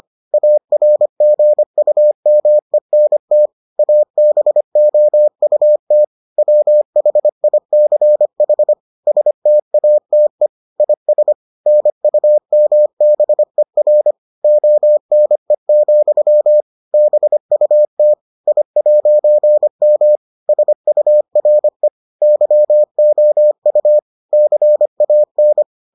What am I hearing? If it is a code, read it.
Morse code, heard as a keyed tone, about 25 wpm, 600 Hz. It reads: ARGUMENT ABOUT WHICH STATE IS NUMBER ONE, BUT I'M SURE YOU CAN